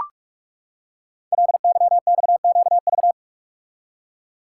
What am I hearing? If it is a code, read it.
Morse code, heard as a keyed tone, about 44 wpm, 690 Hz. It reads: LYXQV
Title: Code Groups - 5 Letters - 45wpm